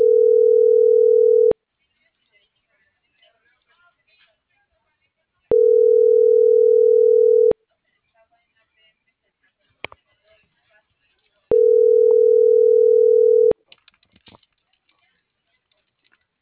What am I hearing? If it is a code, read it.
Ringback tone (North America), 440 + 480 Hz.